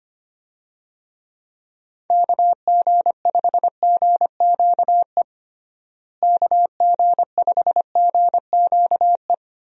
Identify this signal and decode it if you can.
Morse code, heard as a keyed tone, about 25 wpm, 700 Hz. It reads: KG5GQE KG5GQE
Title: USA Call Signs - 6 Characters - Repeated 3x - 25wpm